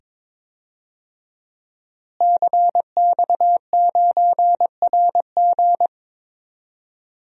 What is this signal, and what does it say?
Morse code, heard as a keyed tone, about 22 wpm, 700 Hz. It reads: CX9RG
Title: DXCC-100 Call Signs - 5 Characters - Repeated 2x - 22wpm